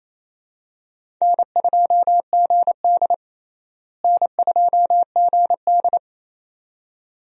Morse code, heard as a keyed tone, about 28 wpm, 700 Hz. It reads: N2GD N2GD
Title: USA Call Signs - 4 Characters - Repeated 2x - 28wpm